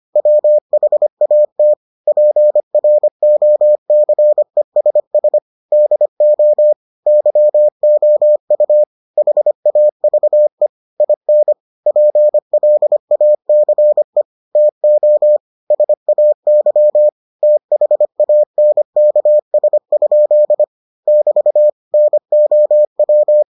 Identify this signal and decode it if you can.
Morse code, heard as a keyed tone, about 25 wpm, 600 Hz. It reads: WHAT PROCESS DO YOU HAVE IN PLACE TO SAY THANKS? <BT> NOW